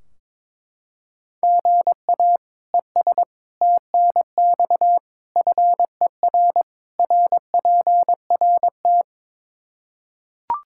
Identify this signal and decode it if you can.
Morse code, heard as a keyed tone, about 22 wpm, 700 Hz. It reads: GA ES TNX FER RPRT
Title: Practice QSO - Exchange 1 - Beginning - 22wpm